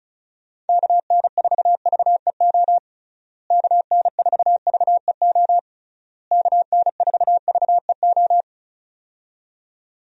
Morse code, heard as a keyed tone, about 36 wpm, 700 Hz. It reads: KN4VEO KN4VEO KN4VEO
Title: USA Call Signs - 6 Characters - Repeated 3x - 35wpm